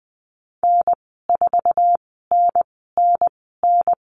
Morse code, heard as a keyed tone, about 20 wpm, spaced as for 14 wpm, 700 Hz. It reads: N4NNN
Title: 048 Lesson 9g - Number 4 - Call Signs 20wpm